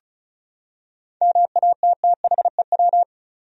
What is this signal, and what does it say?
Morse code, heard as a keyed tone, about 36 wpm, 700 Hz. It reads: MATTHEW
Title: USA Common Names - 35wpm